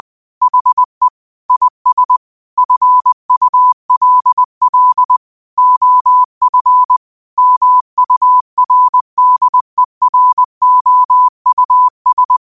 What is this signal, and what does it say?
Morse code, heard as a keyed tone, about 20 wpm, 1000 Hz. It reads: HE IS FULL OF MURDEROUS
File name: qotdfiles / 20wpm